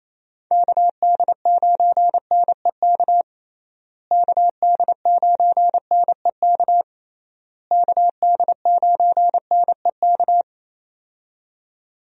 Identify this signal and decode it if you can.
Morse code, heard as a keyed tone, about 28 wpm, 700 Hz. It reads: KD9NEK KD9NEK KD9NEK
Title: USA Call Signs - 6 Characters - Repeated 3x - 28wpm